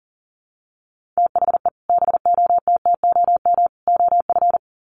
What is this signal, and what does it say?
Morse code, heard as a keyed tone, about 40 wpm, 700 Hz. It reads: THE BOTTOM OF
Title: Sets of 3 Words 40wpm